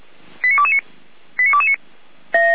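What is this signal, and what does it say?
Five-tone selective call (ZVEI), heard as quick runs of five tones, tones of about 70 ms. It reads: 89229 89229
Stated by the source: ZVEI five-tone selective call 89164, received by SDR, from the feuerwehr-orlamuende.org audio server.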